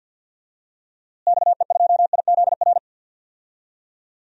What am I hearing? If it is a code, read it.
Morse code, heard as a keyed tone, about 52 wpm, 700 Hz. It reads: XE2IZR
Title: DXCC-100 Call Signs - 6 Characters - Repeated 2x - 50wpm